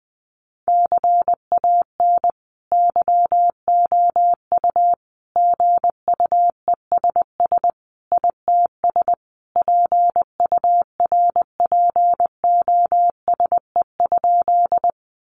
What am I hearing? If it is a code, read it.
Morse code, heard as a keyed tone, about 20 wpm, 700 Hz. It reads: CAN YOU GUESS ITS PURPOSE?
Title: Sentences from Top 1500 Words 20wpm